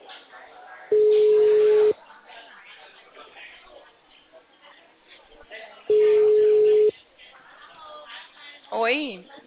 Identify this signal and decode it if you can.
Ringback tone (Europe, 425 Hz), with speech.
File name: out- -2030- - - .22